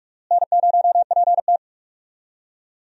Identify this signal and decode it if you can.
Morse code, heard as a keyed tone, about 44 wpm, 700 Hz. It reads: N0PT
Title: USA Call Signs - 4 Characters - Repeated 3x - 45wpm